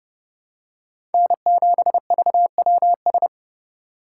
Morse code, heard as a keyed tone, about 30 wpm, 700 Hz. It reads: N7VWS
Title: USA Call Signs - 5 Characters - 2x Character Spacing - 30wpm